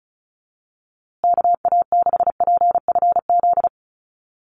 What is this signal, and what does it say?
Morse code, heard as a keyed tone, about 34 wpm, 700 Hz. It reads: KA6PFZ
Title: USA Call Signs Repeated 2x 35wpm